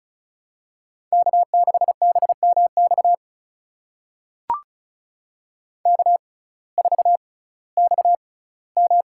Morse code, heard as a keyed tone, about 36 wpm, spaced as for 19 wpm, 700 Hz. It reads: K6BMX K4XM
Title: USA Call Signs - 6x Character Spacing - 35wpm